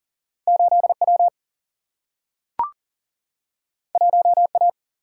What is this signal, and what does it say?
Morse code, heard as a keyed tone, about 40 wpm, 700 Hz. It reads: 8W 1A
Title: Single-Letter Single-Number 40wpm